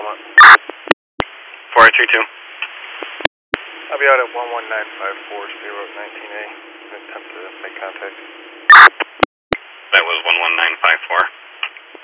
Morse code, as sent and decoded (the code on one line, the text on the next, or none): none